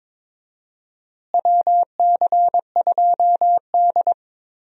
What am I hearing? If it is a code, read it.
Morse code, heard as a keyed tone, about 22 wpm, 700 Hz. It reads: WC2D
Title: USA Call Signs - 4 Characters - Repeated 3x - 22wpm